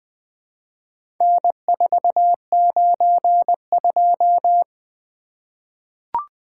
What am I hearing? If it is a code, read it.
Morse code, heard as a keyed tone, about 20 wpm, 700 Hz. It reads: N492
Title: Four Letter-Number Combinations - 20wpm